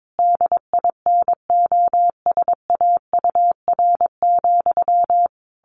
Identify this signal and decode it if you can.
Morse code, heard as a keyed tone, about 22 wpm, 700 Hz. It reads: DINOSAUR,
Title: A Sound Of Thunder 22wpm